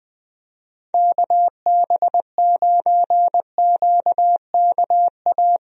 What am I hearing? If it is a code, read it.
Morse code, heard as a keyed tone, about 20 wpm, 700 Hz. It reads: KB9QKA
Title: USA Call Signs - 6 Characters - Repeated 2x - 20wpm